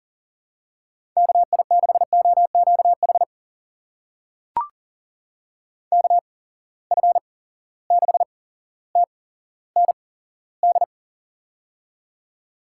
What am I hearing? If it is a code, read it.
Morse code, heard as a keyed tone, about 40 wpm, spaced as for 17 wpm, 700 Hz. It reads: KI6OQH KF6TND
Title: USA Call Signs - 6 Characters - 8x Character Spacing - 40wpm